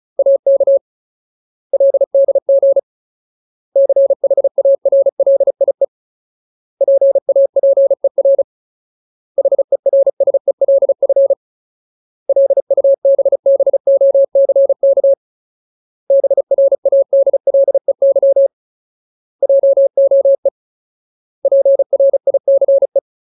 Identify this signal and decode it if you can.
Morse code, heard as a keyed tone, about 34 wpm, 550 Hz. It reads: AK LDG CHARLIE PAPER HERSELF LUBBOCK BRADLEY JOE PRICE
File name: Ultimate Word List / 35WPM Ultimate